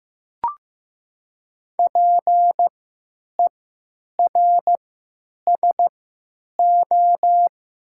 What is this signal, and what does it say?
Morse code, heard as a keyed tone, about 15 wpm, 700 Hz. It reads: P E R S O
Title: 3x Character Spacing - Top 500 Words - 15wpm